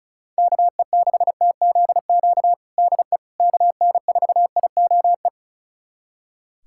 Morse code, heard as a keyed tone, about 36 wpm, 700 Hz. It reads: KE6TZQ DE KN4IOE
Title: USA Callsign DE Callsign 35wpm